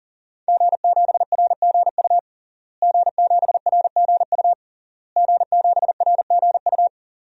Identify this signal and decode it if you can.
Morse code, heard as a keyed tone, about 40 wpm, 700 Hz. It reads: G7RGU G7RGU G7RGU
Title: DXCC-100 Call Signs - 5 Characters - Repeated 3x - 40wpm